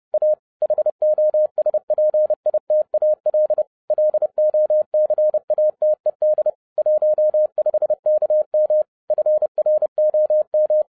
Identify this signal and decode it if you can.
Morse code, heard as a keyed tone, about 30 wpm, 610 Hz. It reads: A HOSPITAL LOCATED 15KM FROM
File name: 2011-07-mp3 / n55 BACK FROM A BIG ACCIDENT 30wpm